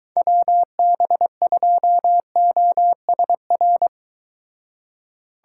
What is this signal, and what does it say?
Morse code, heard as a keyed tone, about 22 wpm, 700 Hz. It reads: WB2OSR